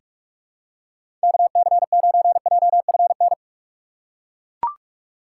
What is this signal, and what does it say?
Morse code, heard as a keyed tone, about 42 wpm, 700 Hz. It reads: KC9JFN
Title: USA Call Signs - 6 Characters - 6x Character Spacing - 45wpm